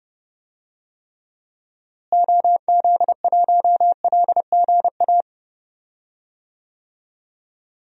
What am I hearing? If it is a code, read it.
Morse code, heard as a keyed tone, about 30 wpm, 700 Hz. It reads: OZ1LGA
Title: DXCC-100 Call Signs - 6 Characters - Repeated 2x - 30wpm